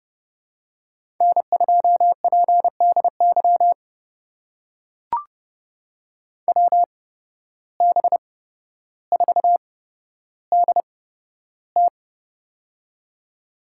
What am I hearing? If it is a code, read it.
Morse code, heard as a keyed tone, about 30 wpm, spaced as for 13 wpm, 700 Hz. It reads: N2PDY WB4DT
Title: USA Call Signs - 5 Characters - 8x Character Spacing - 30wpm